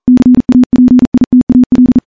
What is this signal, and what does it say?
Morse code, heard as a keyed tone, about 37 wpm, 250 Hz. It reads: CAPITAL
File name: morse-training-40WPM